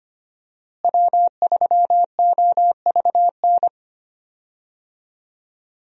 Morse code, heard as a keyed tone, about 25 wpm, 700 Hz. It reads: W3OVN
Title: USA Call Signs - 5 Characters - Repeated 3x - 25wpm